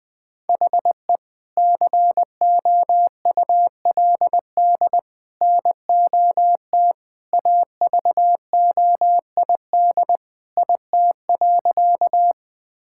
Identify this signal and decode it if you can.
Morse code, heard as a keyed tone, about 20 wpm, 700 Hz. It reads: HE COULD NOT AVOID IT.
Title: Sentences from Top 1000 Words - Review Part 6 - 20wpm